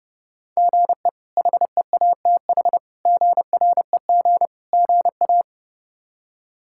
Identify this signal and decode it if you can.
Morse code, heard as a keyed tone, about 32 wpm, 700 Hz. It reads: GE HEATH GREG GA
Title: K1USN SST QSO Practice - 30wpm